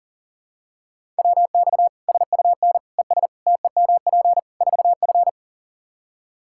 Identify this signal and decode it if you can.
Morse code, heard as a keyed tone, about 37 wpm, 700 Hz. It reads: WX SUN ES TEMP 4F
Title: Practice QSO - Exchange 2 - Info - 40wpm